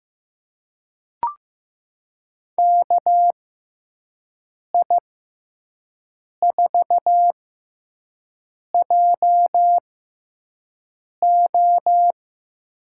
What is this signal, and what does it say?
Morse code, heard as a keyed tone, about 15 wpm, spaced as for 5 wpm, 700 Hz. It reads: KI4JO
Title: USA Call Signs - 6x Character Spacing - 15wpm